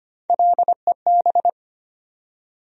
Morse code, heard as a keyed tone, about 25 wpm, 700 Hz. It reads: LEB
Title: USA Call Sign Postfixes - 3 Letters 25wpm